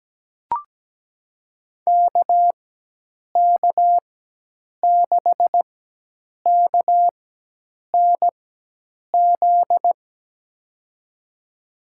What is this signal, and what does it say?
Morse code, heard as a keyed tone, about 17 wpm, spaced as for 8 wpm, 700 Hz. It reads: KK6KNZ
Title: USA Call Signs - 6 Characters - 4x Character Spacing - 17wpm